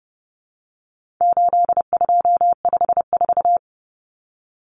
Morse code, heard as a keyed tone, about 30 wpm, 700 Hz. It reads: 8254
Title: Common Cut Numbers - 4-Digit 30wpm